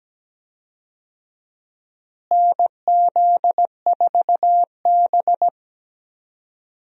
Morse code, heard as a keyed tone, about 17 wpm, 700 Hz. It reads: NZ4B
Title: USA Call Signs - 4 Characters - Repeated 3x - 17wpm